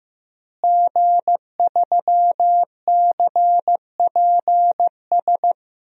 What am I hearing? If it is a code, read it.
Morse code, heard as a keyed tone, about 15 wpm, 700 Hz. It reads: G3CPS
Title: DXCC-100 Call Signs - 5 Characters - Repeated 2x - 15wpm